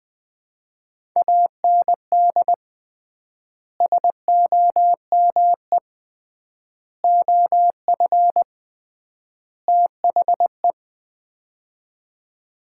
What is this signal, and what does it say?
Morse code, heard as a keyed tone, about 20 wpm, 700 Hz. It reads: AND SOME OF THE
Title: 3x Word-Spacing - Sets of 4 Words - 20wpm